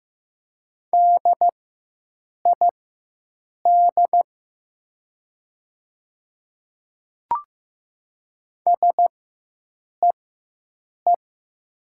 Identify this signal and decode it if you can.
Morse code, heard as a keyed tone, about 16 wpm, spaced as for 7 wpm, 700 Hz. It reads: DID SEE
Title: 4x Character Spacing - Top 100 Words - 15wpm